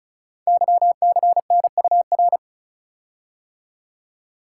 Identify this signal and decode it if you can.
Morse code, heard as a keyed tone, about 36 wpm, 690 Hz. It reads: YCNUR
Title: Code Groups - 5 Letter:Numbers - 35wpm